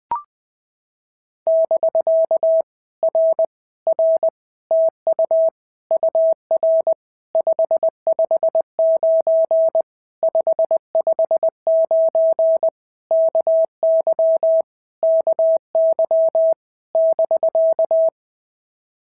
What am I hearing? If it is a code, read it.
Morse code, heard as a keyed tone, about 20 wpm, 650 Hz. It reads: <BK> R R TU UR 559 559 KY KY <BK>
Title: POTA QSO Practice - 20wpm